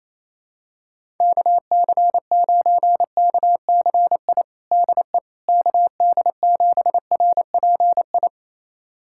Morse code, heard as a keyed tone, about 27 wpm, 700 Hz. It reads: KC9KCI DE KD7RPI
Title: USA Callsign DE Callsign 28wpm